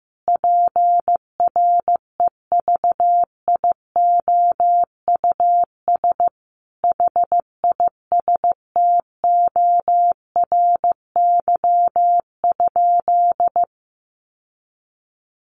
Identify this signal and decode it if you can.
Morse code, heard as a keyed tone, about 15 wpm, 700 Hz. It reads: PREVIOUS HISTORY?
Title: Sentences from Top 1500 Words 15wpm